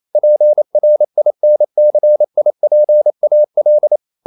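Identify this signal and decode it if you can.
Morse code, heard as a keyed tone, about 28 wpm, 600 Hz. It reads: PRINCIPAL
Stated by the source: Morse code practice at 28 wpm, text of Ken Follett's novel Never in English, from CW med Gnister.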